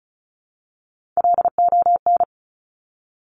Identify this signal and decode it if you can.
Morse code, heard as a keyed tone, about 36 wpm, 700 Hz. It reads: LON
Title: Top Three Letter Combinations in Words 35wpm